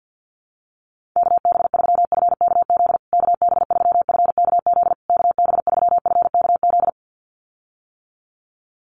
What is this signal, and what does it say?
Morse code, heard as a keyed tone, about 50 wpm, 700 Hz. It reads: KB3FKZ KB3FKZ KB3FKZ
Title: USA Call Signs Repeated 3x 50wpm